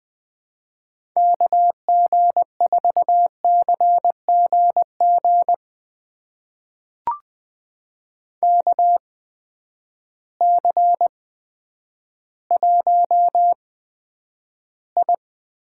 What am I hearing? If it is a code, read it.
Morse code, heard as a keyed tone, about 20 wpm, spaced as for 10 wpm, 700 Hz. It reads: KG4CGG KC1I
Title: USA Call Signs - 6 Characters - 8x Character Spacing - 20wpm